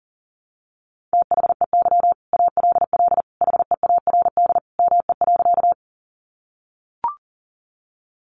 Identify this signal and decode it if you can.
Morse code, heard as a keyed tone, about 40 wpm, 700 Hz. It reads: THEY ALL HEARD ME.
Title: Sentences from Top 400 Words 40wpm